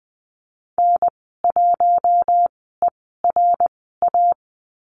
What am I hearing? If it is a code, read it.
Morse code, heard as a keyed tone, about 20 wpm, spaced as for 15 wpm, 700 Hz. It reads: N1ERA
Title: 069 Lesson 12g - Letter d - Call Signs 20wpm